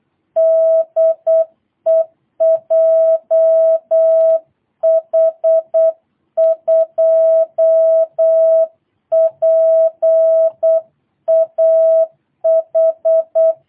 Morse code, heard as a keyed tone, about 8 wpm, 660 Hz. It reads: DE JH2PAH
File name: jh8cbh / 2022cw